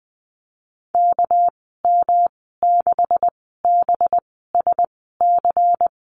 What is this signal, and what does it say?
Morse code, heard as a keyed tone, about 20 wpm, spaced as for 15 wpm, 700 Hz. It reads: KM6BSC
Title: 219 Lesson 35g - Number 0 - Call Signs 20wpm